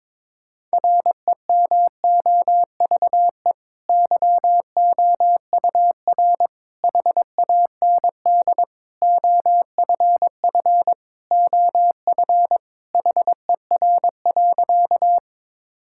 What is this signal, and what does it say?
Morse code, heard as a keyed tone, about 22 wpm, 700 Hz. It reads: REMOVE YOUR HAND OFF OF HER.
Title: Sentences from Top 1000 Words - Review Part 5 - 22wpm